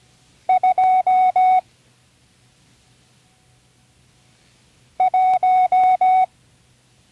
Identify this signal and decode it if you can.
Morse code, heard as a keyed tone, about 17 wpm, 740 Hz. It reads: 2 1
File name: CW Learning Programs / Learning Morse Code (from Tune in the World) / 34 random numbers 1-5